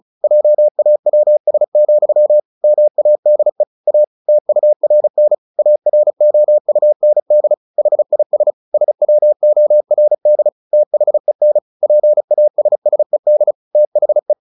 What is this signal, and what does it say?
Morse code, heard as a keyed tone, about 34 wpm, 600 Hz. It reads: JAWS, MADE A TURN AROUND HIS SWORD THEN PASSED THE
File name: the old man and the sea 2 35wpm